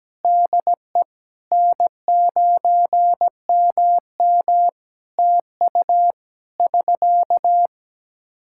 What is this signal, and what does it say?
Morse code, heard as a keyed tone, about 17 wpm, 700 Hz. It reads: DE N9MM TU <SK>
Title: Practice QSO - Exchange 4 - Ending - 17wpm